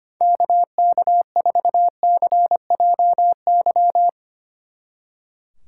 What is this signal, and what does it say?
Morse code, heard as a keyed tone, about 31 wpm, 700 Hz. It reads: KK4CJY
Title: Speed Racing - USA Call Signs 25wpm